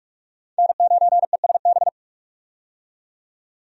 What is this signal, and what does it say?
Morse code, heard as a keyed tone, about 46 wpm, 700 Hz. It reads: N9ESB
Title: USA Call Signs - 5 Characters - 45wpm